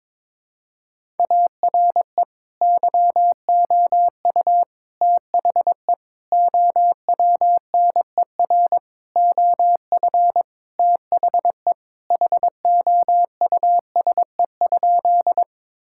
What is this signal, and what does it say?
Morse code, heard as a keyed tone, about 22 wpm, 700 Hz. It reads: ARE YOU THE OWNER OF THE HOUSE?
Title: Sentences from Top 1600 Words 22wpm